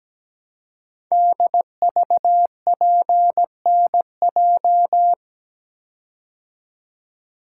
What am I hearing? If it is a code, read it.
Morse code, heard as a keyed tone, about 17 wpm, 700 Hz. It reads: DVPNJ
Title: Code Groups - 5 Letters - 17wpm